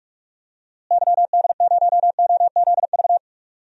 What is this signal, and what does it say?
Morse code, heard as a keyed tone, about 46 wpm, 700 Hz. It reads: YD0OZV
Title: DXCC-100 Call Signs - 6 Characters - Repeated 2x - 45wpm